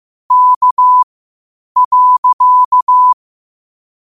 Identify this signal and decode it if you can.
Morse code, heard as a keyed tone, about 15 wpm, spaced as for 8 wpm, 1000 Hz. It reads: K.
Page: Quotes for Sat, 16 Aug 2025 in Morse Code at 8 words per minute.